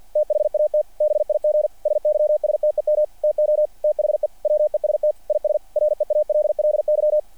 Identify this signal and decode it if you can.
Morse code, heard as a keyed tone, about 48 wpm, 600 Hz. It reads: THAT BIG SYSTEM TO THE WEST IS REALLY
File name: archived / CRSnet